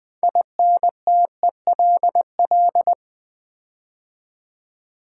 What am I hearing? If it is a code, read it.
Morse code, heard as a keyed tone, about 20 wpm, 700 Hz. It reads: INTELL
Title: Build A Word - 6 Letters At A Time - 12-Letter Words - 20wpm